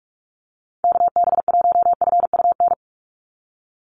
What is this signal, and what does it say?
Morse code, heard as a keyed tone, about 46 wpm, 700 Hz. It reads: KB1FUN
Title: USA Call Signs 45wpm